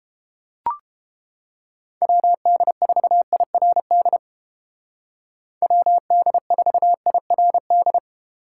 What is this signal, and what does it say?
Morse code, heard as a keyed tone, about 28 wpm, 700 Hz. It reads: WD4IRD WD4IRD